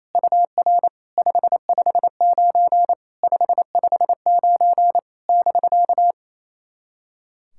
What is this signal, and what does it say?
Morse code, heard as a keyed tone, about 28 wpm, 700 Hz. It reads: UR 559 559 <BK>
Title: SOTA QSO Practice 28wpm